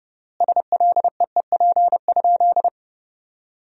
Morse code, heard as a keyed tone, about 30 wpm, 700 Hz. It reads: SLEEP?
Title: 3x Word-Spacing - Sentences from Top 500 Words - 30wpm